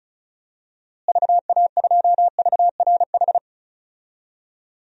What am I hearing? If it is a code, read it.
Morse code, heard as a keyed tone, about 36 wpm, 700 Hz. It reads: VA2VRH
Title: DXCC-100 Call Signs - 6 Characters - Repeated 3x - 35wpm